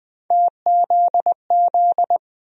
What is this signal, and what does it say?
Morse code, heard as a keyed tone, about 20 wpm, 700 Hz. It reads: TZZ
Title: USA Call Sign Postfixes - 3 Letters 20wpm